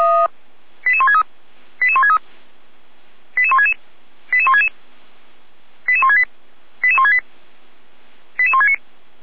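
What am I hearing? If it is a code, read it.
Five-tone selective call (ZVEI), heard as quick runs of five tones, tones of about 70 ms. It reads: 88163 88163 88166 88166 88168 88168 88169
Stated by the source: ZVEI five-tone selective call received on an SDR.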